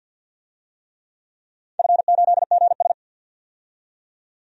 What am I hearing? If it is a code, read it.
Morse code, heard as a keyed tone, about 53 wpm, 700 Hz. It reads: F8GS
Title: DXCC-100 Call Signs - 4 Characters - 50wpm